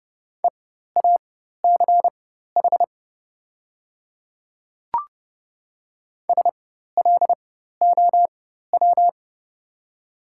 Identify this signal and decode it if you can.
Morse code, heard as a keyed tone, about 30 wpm, spaced as for 14 wpm, 690 Hz. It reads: EACH SLOW
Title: 4x Character Spacing - Top 500 Words - 30wpm